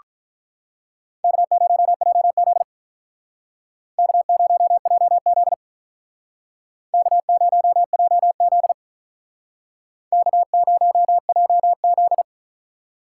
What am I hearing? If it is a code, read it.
Morse code, heard as a keyed tone, about 42 wpm, 700 Hz. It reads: K0JZ K0JZ K0JZ K0JZ
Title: Speed Racing - USA Call Signs 35wpm